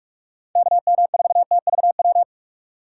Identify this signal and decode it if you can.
Morse code, heard as a keyed tone, about 42 wpm, 690 Hz. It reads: KM4TVW